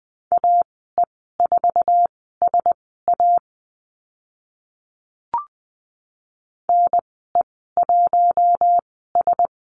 Morse code, heard as a keyed tone, about 20 wpm, spaced as for 14 wpm, 700 Hz. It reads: AE4SA NE1S
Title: 048 Lesson 9g - Number 4 - Call Signs 20wpm